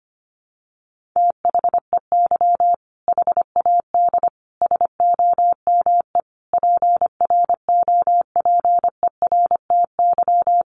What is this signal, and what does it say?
Morse code, heard as a keyed tone, about 25 wpm, 700 Hz. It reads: THEY HAD SOME PROPERTY
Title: Sentences from Top 1500 Words 25wpm